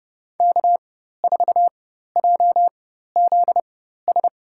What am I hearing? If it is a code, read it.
Morse code, heard as a keyed tone, about 30 wpm, spaced as for 14 wpm, 700 Hz. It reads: K4JZS
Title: USA Call Signs - 5 Characters - 4x Character Spacing - 30wpm